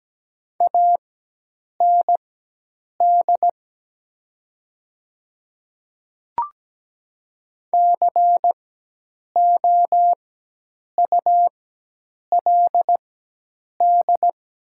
Morse code, heard as a keyed tone, about 17 wpm, spaced as for 8 wpm, 700 Hz. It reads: AND COULD
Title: 4x Character Spacing - Top 100 Words - 17wpm